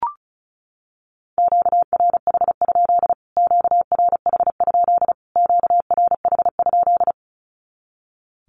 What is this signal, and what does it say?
Morse code, heard as a keyed tone, about 34 wpm, 700 Hz. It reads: QRH? QRH? QRH?
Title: Q-Codes Repeated 3x 35wpm